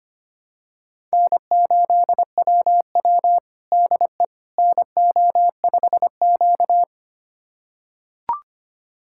Morse code, heard as a keyed tone, about 25 wpm, 700 Hz. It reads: N8WW DE NO5Q
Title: USA Callsign DE Callsign - 4 Characters - 25wpm